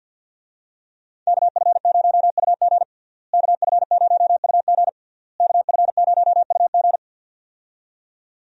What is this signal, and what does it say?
Morse code, heard as a keyed tone, about 50 wpm, 700 Hz. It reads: KF0UG KF0UG KF0UG
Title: USA Call Signs - 5 Characters - Repeated 3x - 50wpm